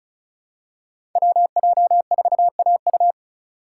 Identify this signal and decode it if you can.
Morse code, heard as a keyed tone, about 36 wpm, 700 Hz. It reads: WJ4AU
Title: Code Groups - 5 Letter:Numbers - 35wpm